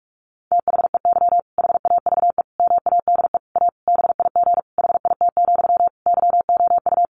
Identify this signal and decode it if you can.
Morse code, heard as a keyed tone, about 44 wpm, 700 Hz. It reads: THEY HAVE MADE A BIG HIT, YOU
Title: Sentences from Top 1200 Words 45wpm